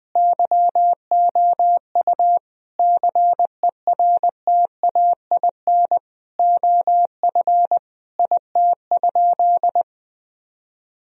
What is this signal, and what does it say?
Morse code, heard as a keyed tone, about 21 wpm, 700 Hz. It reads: YOU CERTAIN OF IT?
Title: Sentences from Top 1000 Words - Review Part 3 - 20wpm